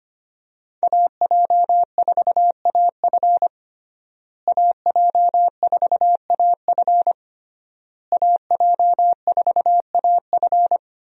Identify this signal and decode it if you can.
Morse code, heard as a keyed tone, about 25 wpm, 700 Hz. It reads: AJ4AF AJ4AF AJ4AF
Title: USA Call Signs - 5 Characters - Repeated 3x - 25wpm